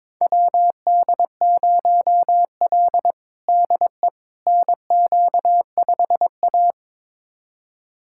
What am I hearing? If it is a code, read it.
Morse code, heard as a keyed tone, about 21 wpm, 700 Hz. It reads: WD0L DE NQ5A
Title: USA Callsign DE Callsign - 4 Characters - 22wpm